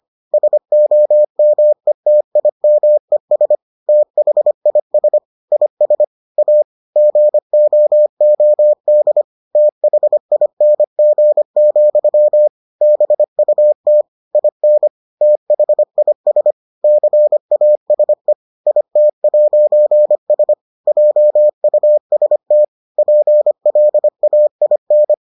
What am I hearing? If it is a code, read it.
Morse code, heard as a keyed tone, about 25 wpm, 600 Hz. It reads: SOMETIMES THIS IS A GOOD THING, BUT IN THIS CASE IT'S JUST PLAIN